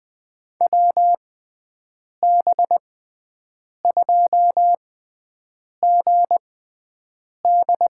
Morse code, heard as a keyed tone, about 20 wpm, spaced as for 7 wpm, 700 Hz. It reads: WB2GD
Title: USA Call Signs - 6 Characters - 6x Character Spacing - 20wpm